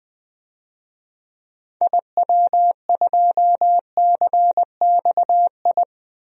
Morse code, heard as a keyed tone, about 20 wpm, 700 Hz. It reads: IW2CXI
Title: DXCC-100 Call Signs - 6 Characters - 20wpm